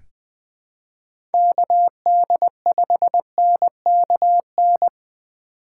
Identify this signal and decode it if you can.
Morse code, heard as a keyed tone, about 20 wpm, 700 Hz. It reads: KD5NKN
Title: USA Call Signs - 6 Characters - 2x Character Spacing - 20wpm